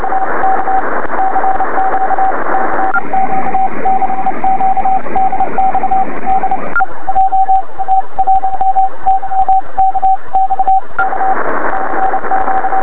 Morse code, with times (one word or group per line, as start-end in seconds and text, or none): none